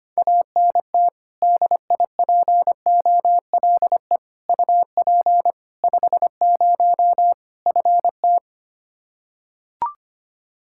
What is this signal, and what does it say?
Morse code, heard as a keyed tone, about 25 wpm, 700 Hz. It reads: ANT DIPOLE UP 50 FT
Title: Practice QSO - Exchange 2 - Info - 25wpm